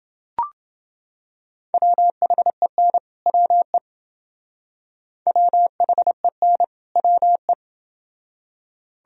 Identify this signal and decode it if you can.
Morse code, heard as a keyed tone, about 28 wpm, 700 Hz. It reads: WHEN WE WHEN WE